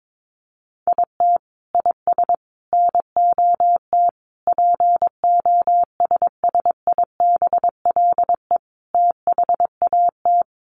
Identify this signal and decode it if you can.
Morse code, heard as a keyed tone, about 22 wpm, 700 Hz. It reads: IT IS NOT POSSIBLE THAT
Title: Sentences from Top 1200 Words 22wpm